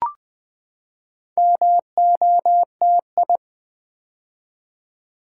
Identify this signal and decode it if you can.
Morse code, heard as a keyed tone, about 20 wpm, 700 Hz. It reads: MOTI
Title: Build A Word - 4 Letters At A Time - 8-Letter Words - 20wpm